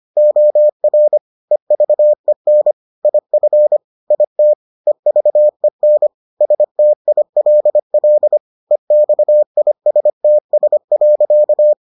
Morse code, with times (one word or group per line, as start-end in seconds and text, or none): 0.2-1.2 OR
1.5-2.7 EVEN
3.0-3.8 IF
4.1-4.5 IT
4.9-6.1 EVEN
6.4-8.4 STILL
8.7-11.7 EXISTS.